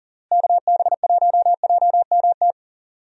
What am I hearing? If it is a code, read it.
Morse code, heard as a keyed tone, about 40 wpm, 700 Hz. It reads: KB1JMT
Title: USA Call Signs - 6 Characters - Repeated 3x - 40wpm